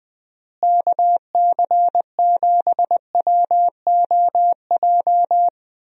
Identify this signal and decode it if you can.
Morse code, heard as a keyed tone, about 20 wpm, 700 Hz. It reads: KC7WOJ
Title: USA Call Signs - 6 Characters - Repeated 3x - 20wpm